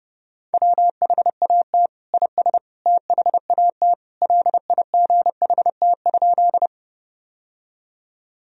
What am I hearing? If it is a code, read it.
Morse code, heard as a keyed tone, about 30 wpm, 700 Hz. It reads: WHAT IS THAT LIGHT?
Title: Sentences from Top 1000 Words - Review Part 1 - 30wpm